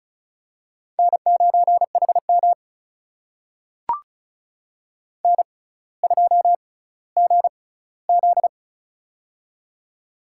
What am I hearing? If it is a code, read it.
Morse code, heard as a keyed tone, about 36 wpm, spaced as for 18 wpm, 700 Hz. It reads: N9HM N2GZ
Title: USA Call Signs - 4 Characters - 6x Character Spacing - 35wpm